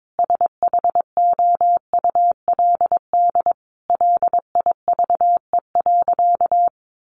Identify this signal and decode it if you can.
Morse code, heard as a keyed tone, about 22 wpm, 700 Hz. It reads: SHOULD LIVE.
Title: Sentences from Top 1100 Words 22wpm